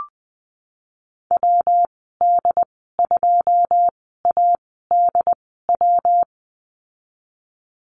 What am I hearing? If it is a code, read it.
Morse code, heard as a keyed tone, about 20 wpm, spaced as for 14 wpm, 700 Hz. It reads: WD2ADW
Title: 128 Lesson 21g - Number 6 - Call Signs 20wpm